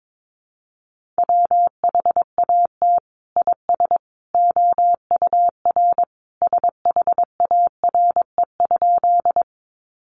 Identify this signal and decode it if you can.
Morse code, heard as a keyed tone, about 22 wpm, 700 Hz. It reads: WHAT IS OUR SHARE?
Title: Sentences from Top 900 Words 22wpm